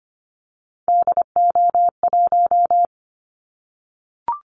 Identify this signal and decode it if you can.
Morse code, heard as a keyed tone, about 25 wpm, 700 Hz. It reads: DO1
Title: Callsign Prefixes for Top 25 DXCC - 25wpm